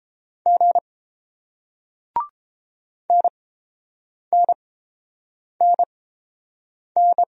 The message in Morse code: G NNNN

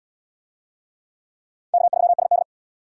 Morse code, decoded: K3SD

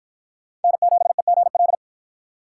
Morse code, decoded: N7EGL

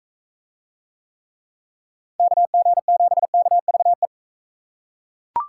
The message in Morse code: KG7KVE

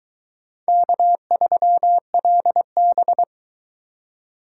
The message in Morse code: K3LB

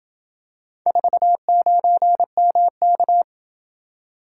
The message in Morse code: 49MK

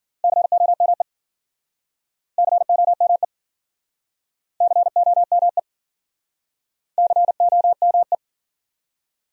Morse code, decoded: COME COME COME COME

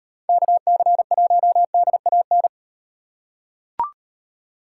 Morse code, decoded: KC1DAN

